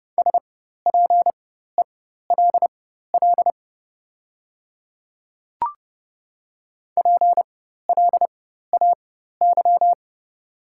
Morse code, heard as a keyed tone, about 30 wpm, spaced as for 14 wpm, 700 Hz. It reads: SPELL PLAY
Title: 4x Character Spacing - Top 300 Words - 30wpm